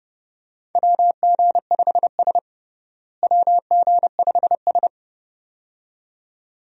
Morse code, heard as a keyed tone, about 30 wpm, 700 Hz. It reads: WG5S WG5S
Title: USA Call Signs - 4 Characters - Repeated 2x - 30wpm